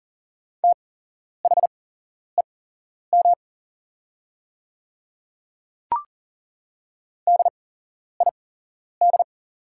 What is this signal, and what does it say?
Morse code, heard as a keyed tone, about 40 wpm, spaced as for 11 wpm, 700 Hz. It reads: THEM DID
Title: 8x Character Spacing - Top 300 Words - 40wpm